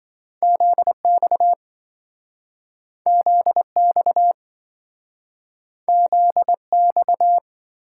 Morse code, heard as a keyed tone, about 26 wpm, 700 Hz. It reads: ZX ZX ZX